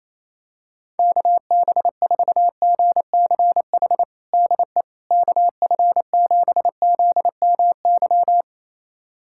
Morse code, heard as a keyed tone, about 28 wpm, 700 Hz. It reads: KB4GCH DE KF7ZMY
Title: USA Callsign DE Callsign 28wpm